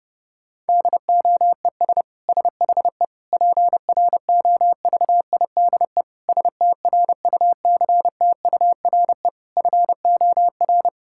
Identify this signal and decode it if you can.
Morse code, heard as a keyed tone, about 30 wpm, 700 Hz. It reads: DOES SHE PROVIDE STRUCTURE FOR
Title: Sentences from Top 1000 Words - Review Part 6 - 30wpm